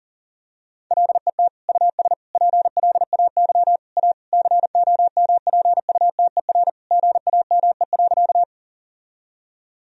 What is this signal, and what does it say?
Morse code, heard as a keyed tone, about 40 wpm, 700 Hz. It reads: LET US PLAY A COMPUTER GAME.
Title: Sentences from Top 1000 Words - Review Part 3 - 40wpm